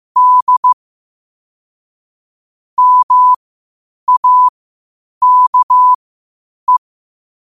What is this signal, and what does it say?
Morse code, heard as a keyed tone, about 15 wpm, spaced as for 8 wpm, 1000 Hz. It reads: D MAKE
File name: qotdfiles / 8wpm